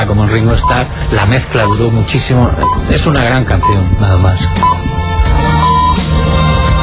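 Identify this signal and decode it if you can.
Radio time signal (time pips), with speech. It is heard from 0.6 to 5.9 s.
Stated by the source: Time signal on Radio Barcelona, Cadena SER, FM, Barcelona.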